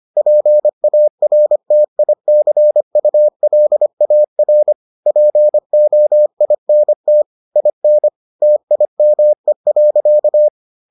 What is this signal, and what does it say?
Morse code, heard as a keyed tone, about 25 wpm, 600 Hz. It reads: PARTICULAR POINT IN TIME.